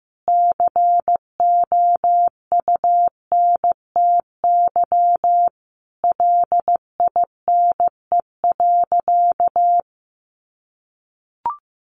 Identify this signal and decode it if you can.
Morse code, heard as a keyed tone, about 15 wpm, 700 Hz. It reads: COUNTY LINE.
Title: Sentences from Top 1200 Words 15wpm